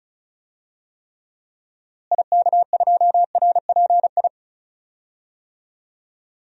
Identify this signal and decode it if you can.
Morse code, heard as a keyed tone, about 36 wpm, 700 Hz. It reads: IK2RPI
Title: DXCC-100 Call Signs - 6 Characters - Repeated 2x - 35wpm